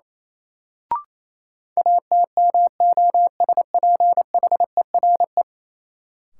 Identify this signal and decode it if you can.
Morse code, heard as a keyed tone, about 28 wpm, 700 Hz. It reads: ATMOSPHERE
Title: Sentences from Top 1600 Words - New Word Drill - 28wpm